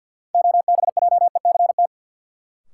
Morse code, heard as a keyed tone, about 48 wpm, 700 Hz. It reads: OBJECT
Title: Sentences from Top 500 Words - New Word Drill - 50wpm